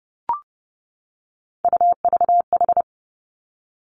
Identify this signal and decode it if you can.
Morse code, heard as a keyed tone, about 30 wpm, 700 Hz. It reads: UVH
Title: Three Challenging Letter-Number Combinations 30wpm